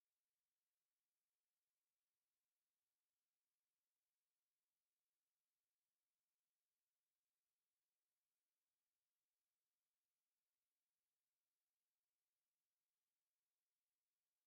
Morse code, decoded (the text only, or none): none